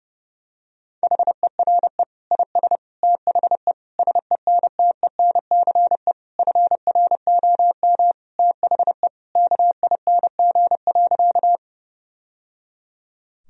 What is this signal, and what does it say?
Morse code, heard as a keyed tone, about 30 wpm, 700 Hz. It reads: HERE IS THE SENTENCE FROM THE KING.
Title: Sentences from Top 1000 Words - Review Part 1 - 30wpm